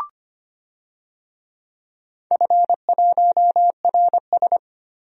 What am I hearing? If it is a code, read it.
Morse code, heard as a keyed tone, about 25 wpm, 700 Hz. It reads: F1RS
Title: DXCC-100 Call Signs - 4 Characters - 25wpm